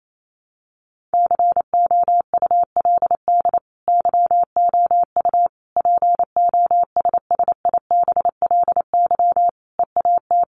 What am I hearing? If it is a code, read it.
Morse code, heard as a keyed tone, about 28 wpm, 700 Hz. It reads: COULD YOU POSSIBLY EAT
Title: Sentences from Top 1300 Words 28wpm